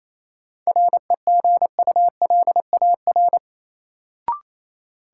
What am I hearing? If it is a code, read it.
Morse code, heard as a keyed tone, about 28 wpm, 700 Hz. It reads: REGULAR
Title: Sentences from Top 900 Words - New Word Drill - 28wpm